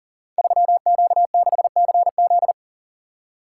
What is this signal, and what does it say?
Morse code, heard as a keyed tone, about 38 wpm, 700 Hz. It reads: 3Q6CZ
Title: Code Groups - 5 Letter:Numbers - 40wpm